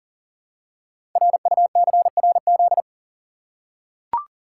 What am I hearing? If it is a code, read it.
Morse code, heard as a keyed tone, about 42 wpm, 690 Hz. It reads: RUCRZ
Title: Code Groups - 5 Letters - 40wpm